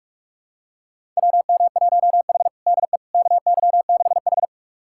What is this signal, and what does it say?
Morse code, heard as a keyed tone, about 46 wpm, 700 Hz. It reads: WM1H DE KY6H
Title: USA Callsign DE Callsign - 4 Characters - 45wpm